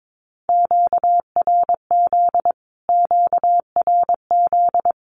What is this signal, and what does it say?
Morse code, heard as a keyed tone, about 22 wpm, 700 Hz. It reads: QRZ QRZ
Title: Q-Codes Repeated 3x 22wpm